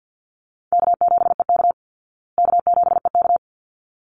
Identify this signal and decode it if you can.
Morse code, heard as a keyed tone, about 50 wpm, 700 Hz. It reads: K7EK K7EK
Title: Easy Call Signs Repeated 2x - 50wpm